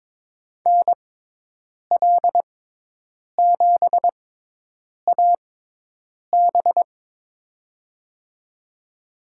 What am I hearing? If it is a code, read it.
Morse code, heard as a keyed tone, about 22 wpm, spaced as for 8 wpm, 700 Hz. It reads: NL7AB
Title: USA Call Signs - 6x Character Spacing - 22wpm